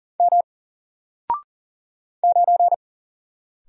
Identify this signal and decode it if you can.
Morse code, heard as a keyed tone, about 40 wpm, 700 Hz. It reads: M 9
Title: Single Character (Letter:Number:Punctuation:Prosign) 40wpm